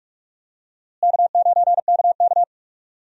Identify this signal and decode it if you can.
Morse code, heard as a keyed tone, about 42 wpm, 700 Hz. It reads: K9KK
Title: USA Call Signs - 4 Characters - Repeated 3x - 45wpm